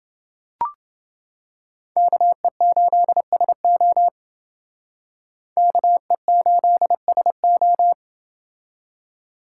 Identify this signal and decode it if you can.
Morse code, heard as a keyed tone, about 29 wpm, 700 Hz. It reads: KE8SO KE8SO